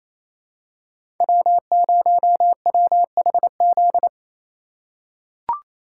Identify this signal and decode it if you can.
Morse code, heard as a keyed tone, about 28 wpm, 700 Hz. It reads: W0WHZ
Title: USA Call Signs - 5 Characters - 2x Character Spacing - 28wpm